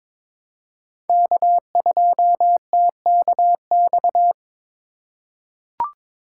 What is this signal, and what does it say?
Morse code, heard as a keyed tone, about 22 wpm, 700 Hz. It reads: K2TKX
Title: USA Call Signs - 5 Characters - 4x Character Spacing - 22wpm